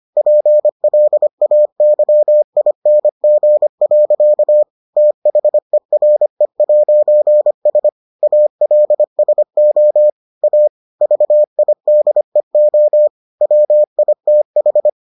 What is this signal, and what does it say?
Morse code, heard as a keyed tone, about 25 wpm, 600 Hz. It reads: PLAYING. THERE'S ALSO A VIDEO WITH